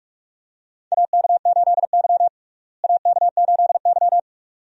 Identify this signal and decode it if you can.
Morse code, heard as a keyed tone, about 44 wpm, 700 Hz. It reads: AK8Y AK8Y
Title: USA Call Signs - 4 Characters - Repeated 2x - 45wpm